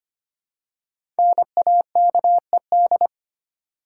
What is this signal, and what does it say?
Morse code, heard as a keyed tone, about 25 wpm, 700 Hz. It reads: NAKED
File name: 5 Letter Words - First 3 Letters Given - 25wpm